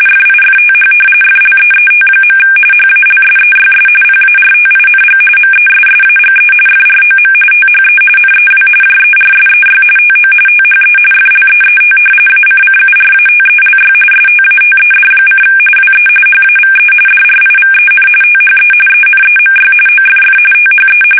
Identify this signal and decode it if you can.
RTTY, about 45.45 baud. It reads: FIFTH FLEET. HALSEYS PARTING MESSAGE FROM THE NEW JERSEY TO THE FLEET ACCURATELY STATED: WE HAVE DRIVEN THE ENEMY OFF THE
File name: fsk850 / battleshipnj